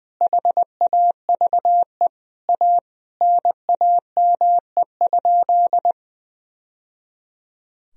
Morse code, heard as a keyed tone, about 20 wpm, 700 Hz. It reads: HAVE A NAME?
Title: Sentences from Top 1000 Words - Review Part 3 - 20wpm